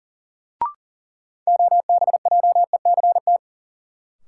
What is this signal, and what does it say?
Morse code, heard as a keyed tone, about 40 wpm, 700 Hz. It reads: OBJECT
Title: Sentences from Top 500 Words - New Word Drill - 40wpm